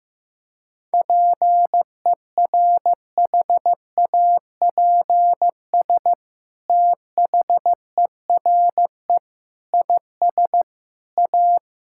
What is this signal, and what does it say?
Morse code, heard as a keyed tone, about 15 wpm, 700 Hz. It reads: PERHAPS THERE IS A
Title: Sentences from Top 1000 Words - Review Part 5 - 15wpm